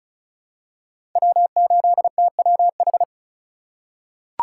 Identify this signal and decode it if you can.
Morse code, heard as a keyed tone, about 36 wpm, 700 Hz. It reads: W8TWH
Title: USA Call Signs - 8x Character Spacing - 35wpm